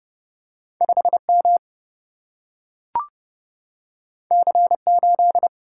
Morse code, heard as a keyed tone, about 30 wpm, 700 Hz. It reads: 5M C8
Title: Single-Letter Single-Number 30wpm